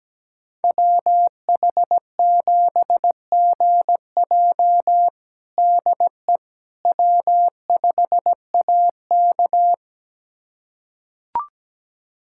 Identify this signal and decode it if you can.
Morse code, heard as a keyed tone, about 17 wpm, 700 Hz. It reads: WH7GJ DE W5AK
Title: USA Callsign DE Callsign 17wpm